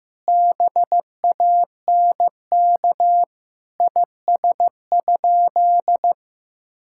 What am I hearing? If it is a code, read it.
Morse code, heard as a keyed tone, about 15 wpm, 700 Hz. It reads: BANK IS?
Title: Sentences from Top 1000 Words - Review Part 6 - 15wpm